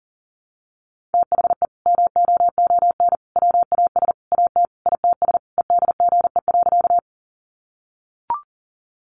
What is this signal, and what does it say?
Morse code, heard as a keyed tone, about 40 wpm, 700 Hz. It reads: THE MOON WAS AT ITS EDGE.
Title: Sentences from Top 1300 Words 40wpm